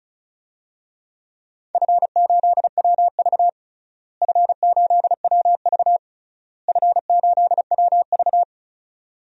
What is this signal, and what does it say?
Morse code, heard as a keyed tone, about 36 wpm, 700 Hz. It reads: F8WV F8WV F8WV
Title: DXCC-100 Call Signs - 4 Characters - Repeated 3x - 35wpm